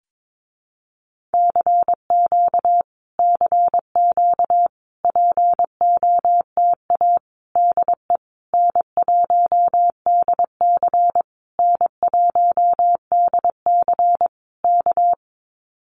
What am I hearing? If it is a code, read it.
Morse code, heard as a keyed tone, about 22 wpm, 700 Hz. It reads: CQ CQ POTA DE N1DC N1DC K
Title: Common QSO Phrases Encore 22wpm